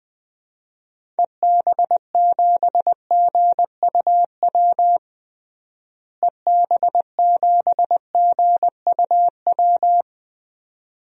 Morse code, heard as a keyed tone, about 20 wpm, 700 Hz. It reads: EB7GUW EB7GUW
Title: DXCC-100 Call Signs - 6 Characters - Repeated 2x - 20wpm